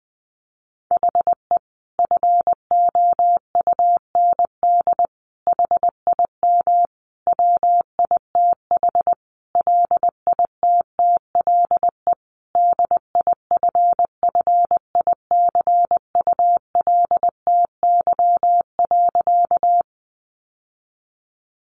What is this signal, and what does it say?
Morse code, heard as a keyed tone, about 20 wpm, 700 Hz. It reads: HE FOUND HIM WITH LITTLE DIFFICULTY.
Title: Sentences from Top 1500 Words 20wpm